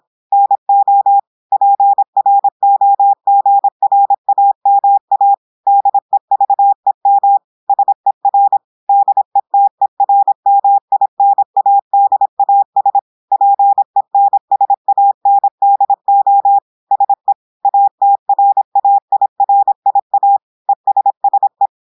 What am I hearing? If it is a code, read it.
Morse code, heard as a keyed tone, about 26 wpm, 800 Hz. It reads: NO PROGRAMA DEVEM SER DETERMINADAS PENSANDO SE ATRAIRIA ESSE
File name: portugais texte en clair 2 vitesse 26 mots minutes